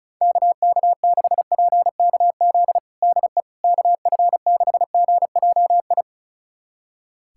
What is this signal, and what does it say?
Morse code, heard as a keyed tone, about 36 wpm, 700 Hz. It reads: KK6PKZ DE KF6GJI
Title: USA Callsign DE Callsign 35wpm